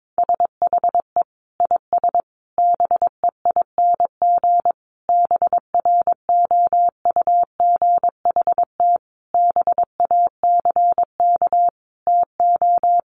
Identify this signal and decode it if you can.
Morse code, heard as a keyed tone, about 22 wpm, 700 Hz. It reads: SHE IS BEING BROUGHT BACK TO
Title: Sentences from Top 600 Words 22wpm